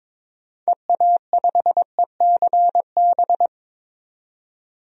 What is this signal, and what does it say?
Morse code, heard as a keyed tone, about 22 wpm, 700 Hz. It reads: EA5ECB
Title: DXCC-100 Call Signs - 6 Characters - 22wpm